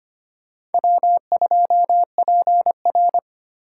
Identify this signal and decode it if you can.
Morse code, heard as a keyed tone, about 30 wpm, 700 Hz. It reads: W2PR
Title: Speed Racing - USA 1x2 Call Signs 25wpm